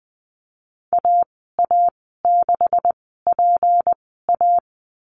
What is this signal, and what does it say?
Morse code, heard as a keyed tone, about 20 wpm, spaced as for 14 wpm, 700 Hz. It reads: AA6PA
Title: 161 Lesson 26g - Letter g - Call Signs 20wpm